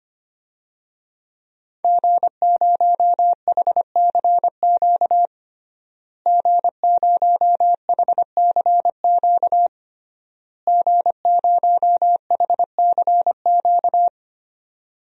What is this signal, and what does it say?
Morse code, heard as a keyed tone, about 25 wpm, 700 Hz. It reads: G0HCQ G0HCQ G0HCQ
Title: DXCC-100 Call Signs - 5 Characters - Repeated 3x - 25wpm